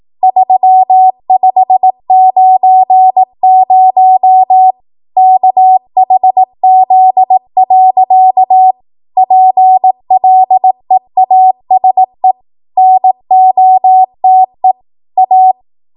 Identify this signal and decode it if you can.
Morse code, heard as a keyed tone, about 18 wpm, 750 Hz. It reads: 3590 KHZ. PLEASE NOTE A
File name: Archive / 18 WPM / 18WPM